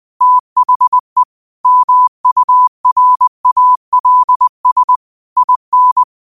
Morse code, heard as a keyed tone, about 20 wpm, 1000 Hz. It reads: THE MURALS IN
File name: qotdfiles / 20wpm